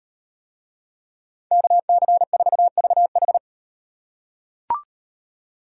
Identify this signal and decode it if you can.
Morse code, heard as a keyed tone, about 36 wpm, 700 Hz. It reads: KC4VH